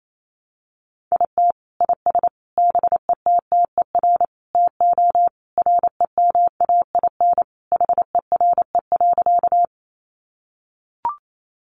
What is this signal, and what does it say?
Morse code, heard as a keyed tone, about 28 wpm, 700 Hz. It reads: IT IS BETTER TO REMAIN HERE.
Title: Sentences from Top 1200 Words 28wpm